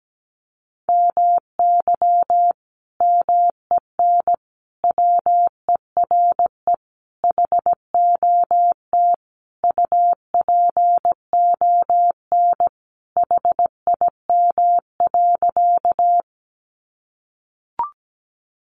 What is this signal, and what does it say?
Morse code, heard as a keyed tone, about 17 wpm, 700 Hz. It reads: MY MEN WERE HOT UPON HIM.
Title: Sentences from Top 1100 Words 17wpm